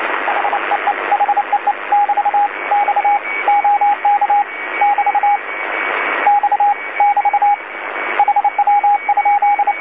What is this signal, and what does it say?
Morse code, heard as a keyed tone, about 28 wpm, 820 Hz, with radio noise. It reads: HEE HEE <BT> X OK <BT> X <BT> HW?
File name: DC-TRCVR / 80-CW